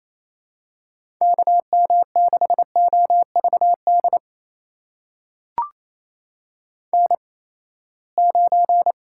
Morse code, heard as a keyed tone, about 28 wpm, spaced as for 19 wpm, 700 Hz. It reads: KM6OVD N9
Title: USA Call Signs - 8x Character Spacing - 28wpm